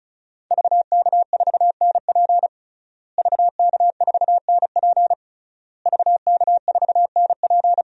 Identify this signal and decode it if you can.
Morse code, heard as a keyed tone, about 36 wpm, 700 Hz. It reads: VK4NP VK4NP VK4NP
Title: DXCC-100 Call Signs - 5 Characters - Repeated 3x - 35wpm